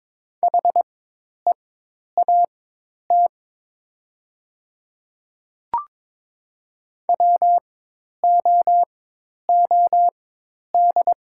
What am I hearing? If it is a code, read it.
Morse code, heard as a keyed tone, about 22 wpm, spaced as for 10 wpm, 700 Hz. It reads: HEAT WOOD
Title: 4x Character Spacing - Top 500 Words - 22wpm